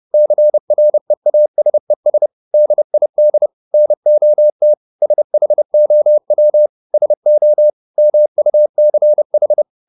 Morse code, heard as a keyed tone, about 30 wpm, 600 Hz. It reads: CREASES DID NOT SHOW SO MUCH
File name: the old man and the sea / the old man and the sea 1 30wpm